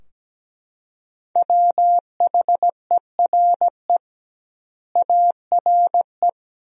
Morse code, heard as a keyed tone, about 17 wpm, 700 Hz. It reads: WHERE ARE
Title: 2x Word-Spacing - Sentences from Top 200 Words - 17wpm